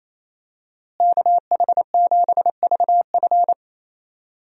Morse code, heard as a keyed tone, about 28 wpm, 700 Hz. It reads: KH7VF
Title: USA Call Signs - 5 Characters - Repeated 2x - 28wpm